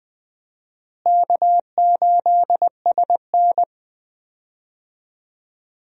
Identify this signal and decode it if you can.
Morse code, heard as a keyed tone, about 20 wpm, 700 Hz. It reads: K8SN
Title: USA Call Signs - 4 Characters - Repeated 2x - 20wpm